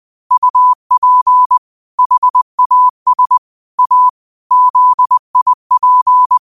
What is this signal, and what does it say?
Morse code, heard as a keyed tone, about 20 wpm, 1000 Hz. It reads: UP HAS A ZIP
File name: qotdfiles / 20wpm